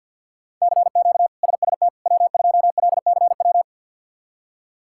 Morse code, heard as a keyed tone, about 49 wpm, 690 Hz. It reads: CQ SST W2LCW
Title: K1USN SST QSO Practice - 50wpm